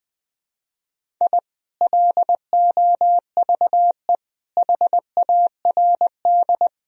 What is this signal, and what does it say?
Morse code, heard as a keyed tone, about 21 wpm, 700 Hz. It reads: I LOVE HARD
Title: Sentences from Top 1000 Words - Review Part 1 - 20wpm